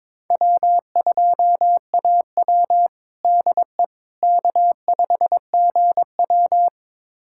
Morse code, heard as a keyed tone, about 22 wpm, 700 Hz. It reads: W2AW DE K5GW
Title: USA Callsign DE Callsign - 4 Characters - 22wpm